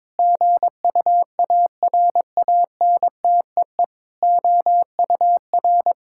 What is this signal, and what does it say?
Morse code, heard as a keyed tone, about 23 wpm, 700 Hz. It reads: GUARANTEE OUR
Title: Sentences from Top 1600 Words 22wpm